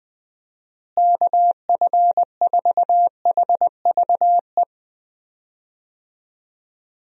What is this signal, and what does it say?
Morse code, heard as a keyed tone, about 21 wpm, 700 Hz. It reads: KF4HVE
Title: USA Call Signs - 6 Characters - Repeated 2x - 20wpm